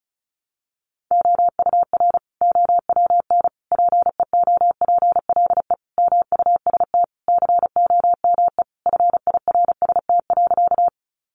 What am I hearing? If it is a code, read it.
Morse code, heard as a keyed tone, about 34 wpm, 700 Hz. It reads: OUR OWN PEOPLE MUST COME FIRST.
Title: Sentences from Top 300 Words 35wpm